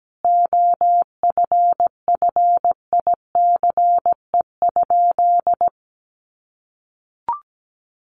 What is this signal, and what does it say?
Morse code, heard as a keyed tone, about 17 wpm, 700 Hz. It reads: OFFICE?
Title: Sentences from Top 1000 Words 17wpm